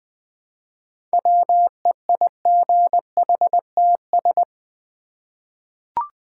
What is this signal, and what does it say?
Morse code, heard as a keyed tone, about 20 wpm, 700 Hz. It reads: WEIGHTS
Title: Easy Words with Hints - 20wpm